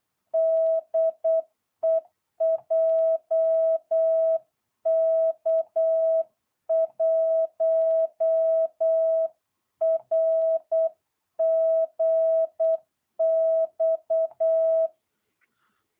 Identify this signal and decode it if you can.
Morse code, heard as a keyed tone, about 8 wpm, 650 Hz. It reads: DE JK1RGX